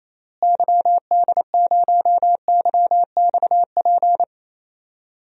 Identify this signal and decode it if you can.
Morse code, heard as a keyed tone, about 28 wpm, 700 Hz. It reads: YD0YXP
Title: DXCC-100 Call Signs - 6 Characters - Repeated 2x - 28wpm